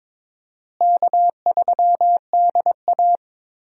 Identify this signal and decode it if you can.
Morse code, heard as a keyed tone, about 22 wpm, 700 Hz. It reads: K3DA
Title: USA Call Signs - 4 Characters - Repeated 3x - 22wpm